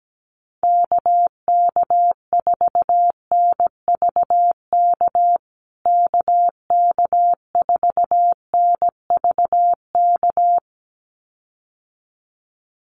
Morse code, heard as a keyed tone, about 17 wpm, 700 Hz. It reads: KK4NVK KK4NVK
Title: USA Call Signs Repeated 2x 17wpm